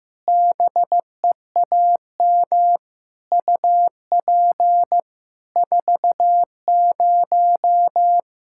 Morse code, heard as a keyed tone, about 15 wpm, 700 Hz. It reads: BEAM UP 40
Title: Practice QSO - Exchange 2 - Info - 15wpm